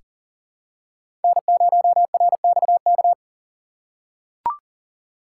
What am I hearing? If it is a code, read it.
Morse code, heard as a keyed tone, about 40 wpm, 700 Hz. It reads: N0RXK
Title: USA Call Signs - 5 Characters - 3x Character Spacing - 40wpm